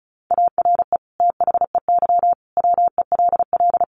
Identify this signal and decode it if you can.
Morse code, heard as a keyed tone, about 34 wpm, 700 Hz. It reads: ARE THEY WELL
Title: Sentences from Top 1300 Words 35wpm